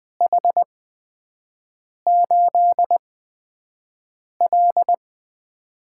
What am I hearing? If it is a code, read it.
Morse code, heard as a keyed tone, about 20 wpm, spaced as for 5 wpm, 700 Hz. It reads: H8L
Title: USA Call Signs - 8x Character Spacing - 20wpm